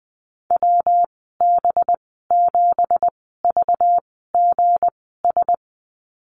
Morse code, heard as a keyed tone, about 19 wpm, spaced as for 14 wpm, 700 Hz. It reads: WB7VGS
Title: 219 Lesson 35g - Number 0 - Call Signs 20wpm